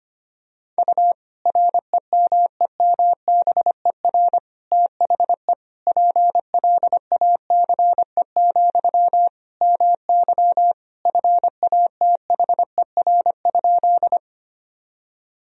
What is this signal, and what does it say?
Morse code, heard as a keyed tone, about 25 wpm, 700 Hz. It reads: U REMEMBER THE PLACE, MY FATHER?
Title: Sentences from Top 400 Words and Abbreviations 25wpm